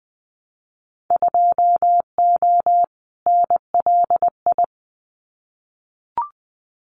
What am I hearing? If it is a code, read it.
Morse code, heard as a keyed tone, about 20 wpm, 700 Hz. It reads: 2O NLI
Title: Winter Field Day Exchanges 20wpm